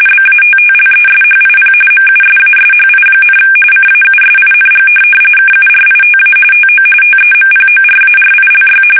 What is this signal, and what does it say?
RTTY, about 45.45 baud. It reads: BATTLESHIP MUSASHI, CAUSING KURITA TO TEMPORARILY
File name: fsk850 / USS-Massachusetts